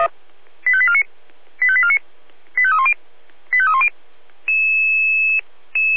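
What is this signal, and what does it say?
Five-tone selective call (ZVEI), heard as quick runs of five tones, tones of about 70 ms. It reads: 85849 85849 85310 85310
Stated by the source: ZVEI five-tone call 85870 received on an SDR.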